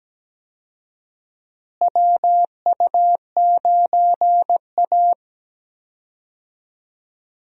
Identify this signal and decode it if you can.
Morse code, heard as a keyed tone, about 17 wpm, 700 Hz. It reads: WU9A
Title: USA Call Signs - 4 Characters - Repeated 2x - 17wpm